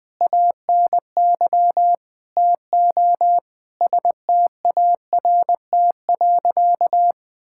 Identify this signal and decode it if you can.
Morse code, heard as a keyed tone, about 21 wpm, 700 Hz. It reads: ANY TO START.
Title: Sentences with 7-Word Phrases - 20wpm